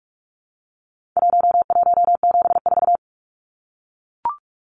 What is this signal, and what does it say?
Morse code, heard as a keyed tone, about 44 wpm, 700 Hz. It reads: 1174
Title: Numbers - Contest Style - 4-Digit 45wpm